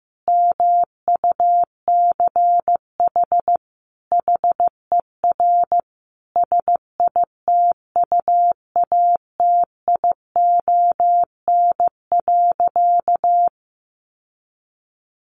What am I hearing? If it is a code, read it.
Morse code, heard as a keyed tone, about 15 wpm, 700 Hz. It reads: MUCH HER SITUATION.
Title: Sentences from Top 1200 Words 15wpm